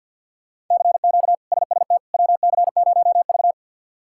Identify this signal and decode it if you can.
Morse code, heard as a keyed tone, about 52 wpm, 700 Hz. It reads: CQ SST WC0V
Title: K1USN SST QSO Practice - 50wpm